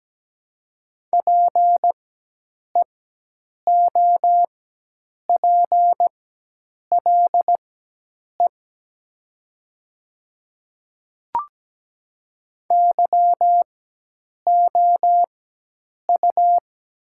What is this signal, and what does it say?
Morse code, heard as a keyed tone, about 17 wpm, spaced as for 8 wpm, 700 Hz. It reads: PEOPLE YOU
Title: 4x Character Spacing - Top 100 Words - 17wpm